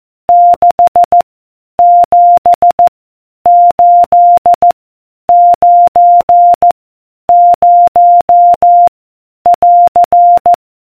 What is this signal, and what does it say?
Morse code, heard as a keyed tone, about 14 wpm, 700 Hz. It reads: 6 7 8 9 0 <AR>